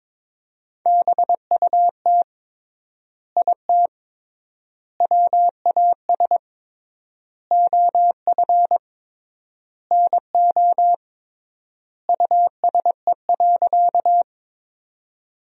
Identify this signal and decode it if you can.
Morse code, heard as a keyed tone, about 21 wpm, 700 Hz. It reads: BUT IT WAS OF NO USE.
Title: 3x Word-Spacing - Sentences from Top 100 Words - 22wpm